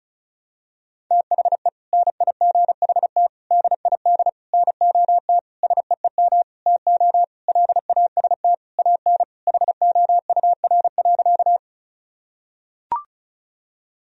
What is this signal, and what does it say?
Morse code, heard as a keyed tone, about 36 wpm, 700 Hz. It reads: THE NIGHT DID NOT SEEM TO LAST AN HOUR.
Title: Sentences from Top 1000 Words - Review Part 1 - 35wpm